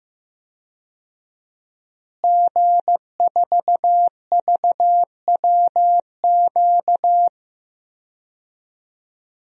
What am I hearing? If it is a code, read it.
Morse code, heard as a keyed tone, about 15 wpm, 700 Hz. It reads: G4VWQ
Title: DXCC-100 Call Signs - 5 Characters - Repeated 2x - 15wpm